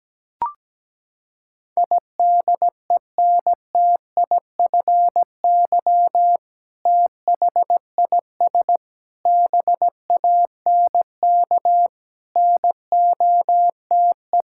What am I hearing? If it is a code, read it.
Morse code, heard as a keyed tone, about 17 wpm, 700 Hz. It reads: IDENTIFY THIS BANK NOTE
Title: Sentences from Top 1000 Words - Review Part 6 - 17wpm